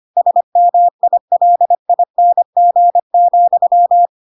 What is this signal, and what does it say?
Morse code, heard as a keyed tone, about 25 wpm, 700 Hz. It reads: SMILING,
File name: Pinocchio / Pinocchio ch5 25 25 ik1hge